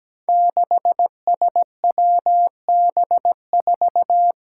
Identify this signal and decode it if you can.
Morse code, heard as a keyed tone, about 17 wpm, 700 Hz. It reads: 6SWB4
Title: Five Challenging Letter-Number Combinations 17wpm